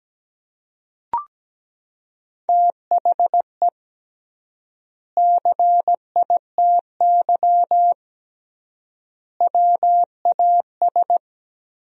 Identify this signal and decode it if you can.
Morse code, heard as a keyed tone, about 17 wpm, 700 Hz. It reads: THE CITY WAS
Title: 3x Word-Spacing - Sentences from Top 500 Words - 17wpm